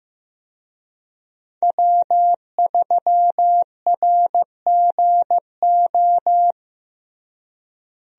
Morse code, heard as a keyed tone, about 15 wpm, 700 Hz. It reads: W3RGO
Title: USA Call Signs - 5 Characters - Repeated 3x - 15wpm